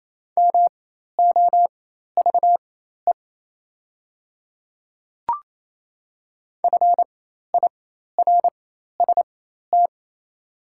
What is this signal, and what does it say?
Morse code, heard as a keyed tone, about 28 wpm, spaced as for 13 wpm, 690 Hz. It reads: MOVE FIRST
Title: 4x Character Spacing - Top 200 Words - 28wpm